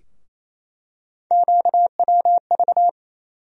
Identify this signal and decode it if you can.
Morse code, heard as a keyed tone, about 27 wpm, 700 Hz. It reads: QWV
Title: USA Call Sign Postfixes - 3 Letters 28wpm